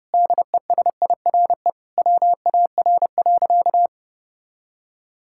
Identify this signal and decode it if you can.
Morse code, heard as a keyed tone, about 32 wpm, 700 Hz. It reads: DESIRE WAR.
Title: Sentences from Top 1600 Words 30wpm